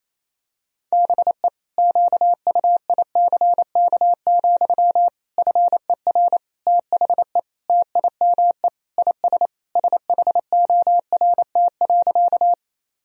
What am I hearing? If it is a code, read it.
Morse code, heard as a keyed tone, about 28 wpm, 700 Hz. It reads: BE QUICK, FER THE TIME IS SHORT.
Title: Sentences from Top 500 Words and Abbreviations 28wpm